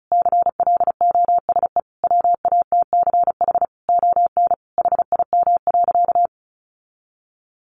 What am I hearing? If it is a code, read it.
Morse code, heard as a keyed tone, about 34 wpm, 700 Hz. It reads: CLOSE WATCH ON HIM.
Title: Sentences from Top 1300 Words 35wpm